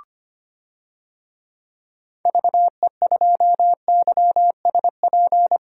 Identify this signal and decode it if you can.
Morse code, heard as a keyed tone, about 25 wpm, 700 Hz. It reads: VE2YSP
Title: DXCC-100 Call Signs - 6 Characters - Repeated 2x - 25wpm